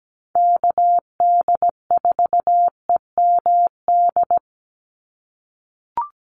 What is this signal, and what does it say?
Morse code, heard as a keyed tone, about 17 wpm, 700 Hz. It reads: KD4EMD
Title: USA Call Signs 17wpm